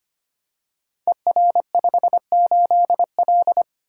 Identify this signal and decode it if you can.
Morse code, heard as a keyed tone, about 25 wpm, 700 Hz. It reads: ER58L
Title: Code Groups - 5 Letter:Numbers - 25wpm